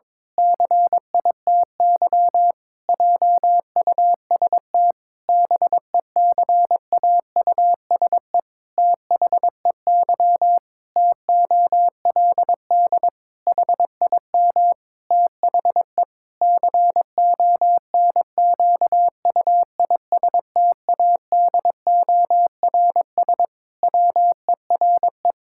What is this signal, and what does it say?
Morse code, heard as a keyed tone, about 22 wpm, 700 Hz. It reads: CITY JUST BECAUSE THEY TOLD HIM THE CONQUISTADORS WERE
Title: Pangram Challenge v2 - 22wpm